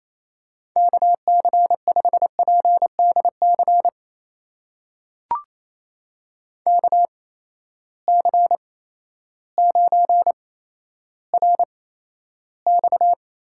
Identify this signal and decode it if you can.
Morse code, heard as a keyed tone, about 28 wpm, spaced as for 13 wpm, 700 Hz. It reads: KC5PDC KC9RX
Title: USA Call Signs - 6 Characters - 8x Character Spacing - 28wpm